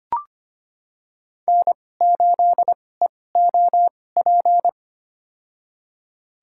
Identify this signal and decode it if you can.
Morse code, heard as a keyed tone, about 25 wpm, spaced as for 18 wpm, 700 Hz. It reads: N8EOP
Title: USA Call Signs - 2x Character Spacing - 25wpm